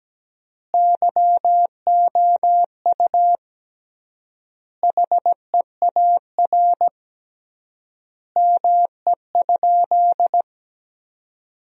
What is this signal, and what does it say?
Morse code, heard as a keyed tone, about 17 wpm, 700 Hz. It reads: YOU HEAR ME?
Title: 3x Word-Spacing - Sentences from Top 300 Words - 17wpm